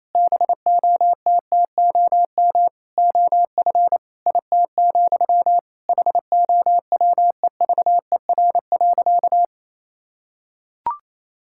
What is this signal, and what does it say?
Morse code, heard as a keyed tone, about 28 wpm, 700 Hz. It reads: BOTTOM OF IT, HOWEVER.
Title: Sentences from Top 1000 Words - Review Part 6 - 28wpm